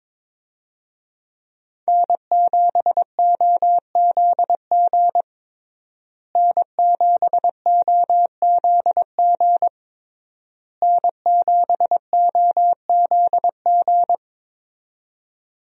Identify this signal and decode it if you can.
Morse code, heard as a keyed tone, about 21 wpm, 700 Hz. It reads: N7OZG N7OZG N7OZG
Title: USA Call Signs - 5 Characters - Repeated 3x - 22wpm